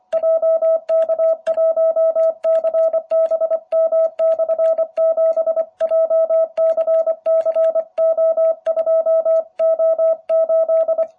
Morse code, heard as a keyed tone, about 25 wpm, 650 Hz. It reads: JK1CBM/7 JCCO2O8